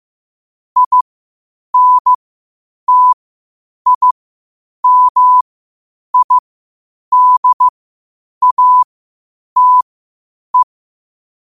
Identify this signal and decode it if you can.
Morse code, heard as a keyed tone, about 15 wpm, spaced as for 8 wpm, 1000 Hz. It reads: INTIMIDATE
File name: qotdfiles / 8wpm